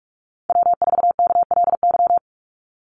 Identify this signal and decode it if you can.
Morse code, heard as a keyed tone, about 46 wpm, 700 Hz. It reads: W4KLY
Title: Call Signs - 45wpm